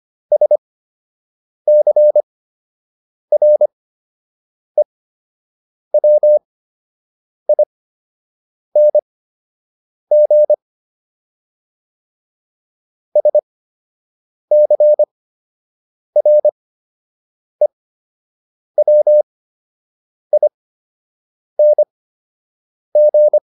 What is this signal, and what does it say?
Morse code, heard as a keyed tone, about 25 wpm, spaced as for 7 wpm, 600 Hz. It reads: SCREWING SCREWING